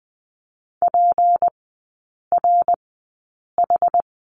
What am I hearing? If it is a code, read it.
Morse code, heard as a keyed tone, about 20 wpm, spaced as for 8 wpm, 700 Hz. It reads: PRH
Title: 191 Lesson 31e - Letter v - Three Characters 20wpm